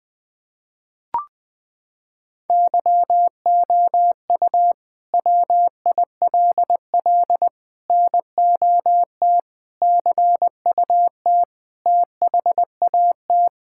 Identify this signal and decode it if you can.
Morse code, heard as a keyed tone, about 20 wpm, 700 Hz. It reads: YOU WILL NOT CUT THAT
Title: Sentences from Top 1000 Words - Review Part 6 - 20wpm